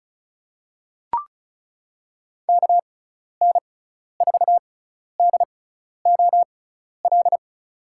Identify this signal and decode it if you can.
Morse code, heard as a keyed tone, about 34 wpm, spaced as for 12 wpm, 700 Hz. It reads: KN4DOL
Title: USA Call Signs - 6x Character Spacing - 35wpm